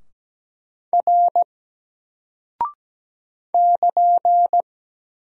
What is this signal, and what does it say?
Morse code, heard as a keyed tone, about 17 wpm, 700 Hz. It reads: R <KN>
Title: Single Character (Letter:Number:Punctuation:Prosign) 17wpm